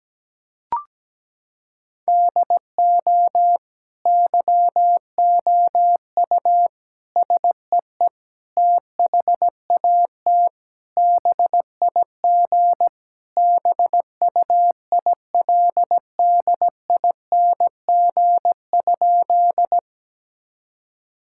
Morse code, heard as a keyed tone, about 17 wpm, 700 Hz. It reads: DO YOU SEE THAT BIG BUILDING?
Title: Sentences from Top 1000 Words - Review Part 5 - 17wpm